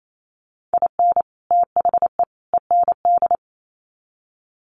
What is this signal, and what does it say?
Morse code, heard as a keyed tone, about 28 wpm, 700 Hz. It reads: IN THE END
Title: Sets of 3 Words 28wpm